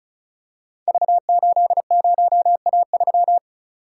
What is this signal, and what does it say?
Morse code, heard as a keyed tone, about 36 wpm, 700 Hz. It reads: V80A3
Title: Code Groups - 5 Letter:Numbers - 35wpm